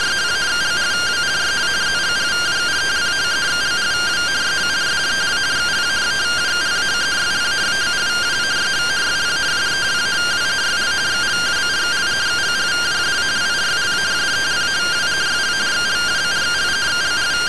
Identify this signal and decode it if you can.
Two-tone teleprinter signal, about 170 Hz shift.